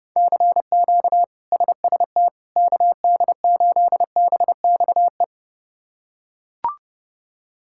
Morse code, heard as a keyed tone, about 30 wpm, 700 Hz. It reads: CQ SST KD8BXE
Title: K1USN SST QSO Practice - 30wpm